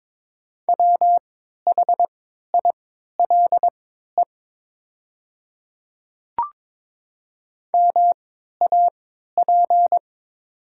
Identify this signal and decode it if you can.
Morse code, heard as a keyed tone, about 22 wpm, spaced as for 12 wpm, 700 Hz. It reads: WHILE MAP
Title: 3x Character Spacing - Top 500 Words - 22wpm